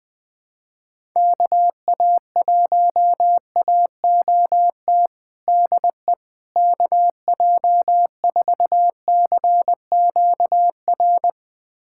Morse code, heard as a keyed tone, about 20 wpm, 700 Hz. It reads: KA1AOT DE KJ4CQR
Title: USA Callsign DE Callsign 20wpm